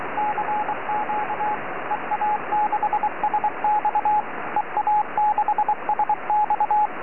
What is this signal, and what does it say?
Morse code, heard as a keyed tone, about 23 wpm, 840 Hz, with radio noise. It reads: CQ EA6SX EA6SX
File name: ea6sx 4m